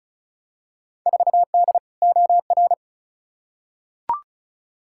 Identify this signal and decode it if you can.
Morse code, heard as a keyed tone, about 36 wpm, 700 Hz. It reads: 4D OR
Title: Field Day Exchanges Encore - 35wpm